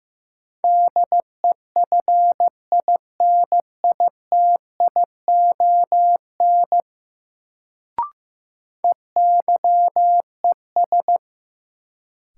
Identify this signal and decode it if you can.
Morse code, heard as a keyed tone, about 16 wpm, 700 Hz. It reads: DEFINITION EYES
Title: Sentences from Top 1000 Words - Review Part 4 - New Word Drill - 15wpm